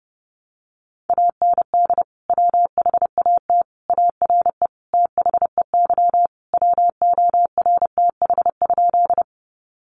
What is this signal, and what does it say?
Morse code, heard as a keyed tone, about 30 wpm, 700 Hz. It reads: AND WHAT ARE THEY WORTH?
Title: Sentences from Top 800 Words 30wpm